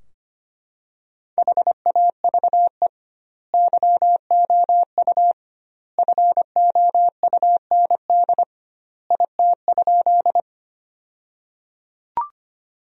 Morse code, heard as a keyed tone, about 25 wpm, 700 Hz. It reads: HAVE YOU FOUND IT?
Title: 2x Word-Spacing - Sentences from Top 300 Words - 25wpm